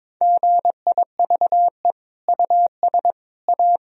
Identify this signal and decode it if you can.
Morse code, heard as a keyed tone, about 22 wpm, 700 Hz. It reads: GIVE US A
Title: Sentences from Top 1000 Words - Review Part 4 - 22wpm